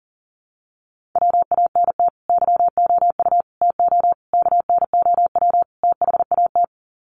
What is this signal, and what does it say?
Morse code, heard as a keyed tone, about 38 wpm, 700 Hz. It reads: WANT YOU TO KNOW THAT
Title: Sets of 5 Words Encore 40wpm